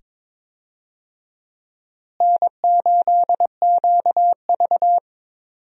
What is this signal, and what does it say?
Morse code, heard as a keyed tone, about 22 wpm, 700 Hz. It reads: N8QV
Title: USA Call Signs - 4 Characters - Repeated 2x - 22wpm